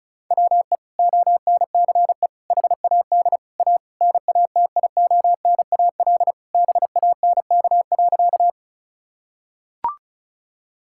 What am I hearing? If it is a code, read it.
Morse code, heard as a keyed tone, about 36 wpm, 700 Hz. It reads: WE ONCE HAD A NATIONAL BANK.
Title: Sentences from Top 1000 Words - Review Part 6 - 35wpm